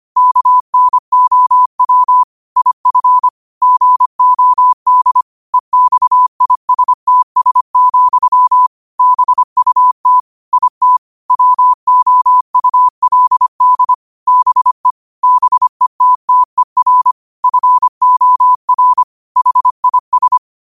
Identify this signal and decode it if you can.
Morse code, heard as a keyed tone, about 25 wpm, 1000 Hz. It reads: KNOW IF GOD EXISTS, BUT IT WOULD BE BETTER FOR HIS